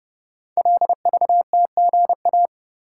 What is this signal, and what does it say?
Morse code, heard as a keyed tone, about 30 wpm, 700 Hz. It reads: LVTGA
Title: Code Groups - 5 Letters - 30wpm